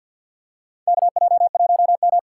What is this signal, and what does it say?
Morse code, heard as a keyed tone, about 51 wpm, 700 Hz. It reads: KJ1M
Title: USA Call Signs - 4 Characters - 3x Character Spacing - 50wpm